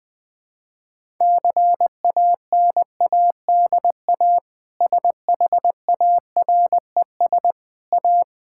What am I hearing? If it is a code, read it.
Morse code, heard as a keyed tone, about 21 wpm, 700 Hz. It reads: CANADA SHARES A
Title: Sentences with 4-Word Phrases - 20wpm